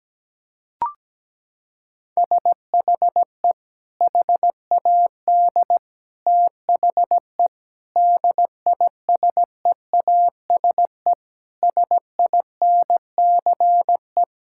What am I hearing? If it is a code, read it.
Morse code, heard as a keyed tone, about 17 wpm, 700 Hz. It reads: SHE HAD THE DISEASE SINCE
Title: Sentences from Top 1000 Words - Review Part 6 - 17wpm